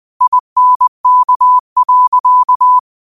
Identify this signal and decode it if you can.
Morse code, heard as a keyed tone, about 20 wpm, 1000 Hz. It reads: INK.
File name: qotdfiles / 20wpm